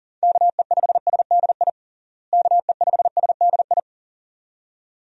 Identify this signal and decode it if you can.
Morse code, heard as a keyed tone, about 40 wpm, 700 Hz. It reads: KE5SDI KE5SDI
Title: USA Call Signs - 6 Characters - Repeated 2x - 40wpm